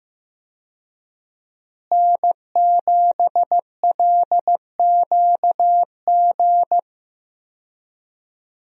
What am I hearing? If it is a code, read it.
Morse code, heard as a keyed tone, about 15 wpm, 700 Hz. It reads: N7LQG
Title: USA Call Signs - 5 Characters - Repeated 3x - 15wpm